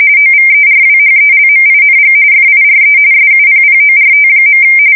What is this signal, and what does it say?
RTTY, about 45.45 baud. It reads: OVERWHELMING FIRE SUPPORT.
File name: fsk170 / battleshipnj